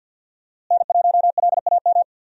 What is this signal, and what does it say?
Morse code, heard as a keyed tone, about 46 wpm, 700 Hz. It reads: N1LAM
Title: USA Call Signs - 5 Characters - Repeated 3x - 50wpm